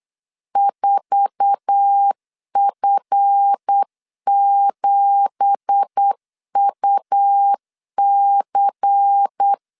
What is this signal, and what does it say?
Morse code, heard as a keyed tone, about 8 wpm, 800 Hz. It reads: 4F7UC